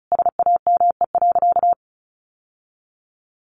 Morse code, heard as a keyed tone, about 34 wpm, 700 Hz. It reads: SAME.
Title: Sentences from Top 1000 Words 35wpm